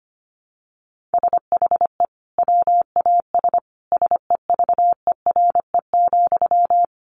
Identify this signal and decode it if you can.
Morse code, heard as a keyed tone, about 25 wpm, 700 Hz. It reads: SHE WAS SEVERE,
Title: Sentences from Top 1400 Words 25wpm